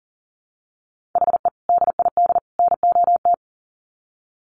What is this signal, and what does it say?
Morse code, heard as a keyed tone, about 40 wpm, 700 Hz. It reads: HE DID NOT
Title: Sets of 3 Words 40wpm